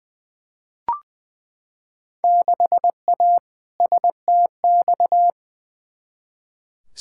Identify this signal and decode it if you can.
Morse code, heard as a keyed tone, about 20 wpm, 700 Hz. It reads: 6A STX
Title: Field Day Exchanges - 20wpm